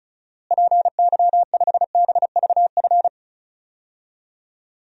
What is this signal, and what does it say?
Morse code, heard as a keyed tone, about 34 wpm, 700 Hz. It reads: PY5BVF
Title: DXCC-100 Call Signs - 6 Characters - Repeated 3x - 35wpm